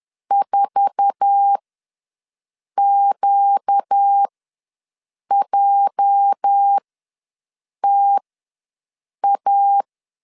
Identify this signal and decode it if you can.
Morse code, heard as a keyed tone, about 10 wpm, spaced as for 6 wpm, 800 Hz. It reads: 4QJTA
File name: LEZIONI / 8 YJ45 / 8 YJ4 ws 10wpm 4020